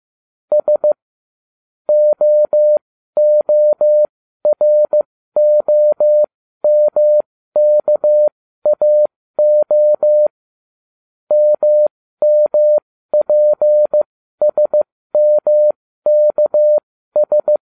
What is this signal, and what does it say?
Morse code, heard as a keyed tone, about 15 wpm, spaced as for 12 wpm, 600 Hz. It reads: S OOROMKAO MMPSMKS
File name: cw / ubacw09 1